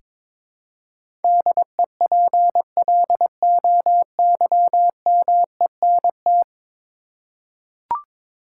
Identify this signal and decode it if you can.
Morse code, heard as a keyed tone, about 22 wpm, 700 Hz. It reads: DEPLOYMENT
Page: morse-code-ninja-practice-22wpm